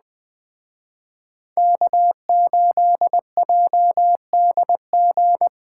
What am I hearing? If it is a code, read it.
Morse code, heard as a keyed tone, about 20 wpm, 700 Hz. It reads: K8JDG
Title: Speed Racing - USA 1x3 Call Signs 17wpm